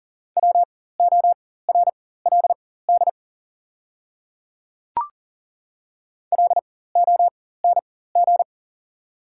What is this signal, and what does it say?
Morse code, heard as a keyed tone, about 40 wpm, spaced as for 19 wpm, 690 Hz. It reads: WORLD LONG
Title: 4x Character Spacing - Top 300 Words - 40wpm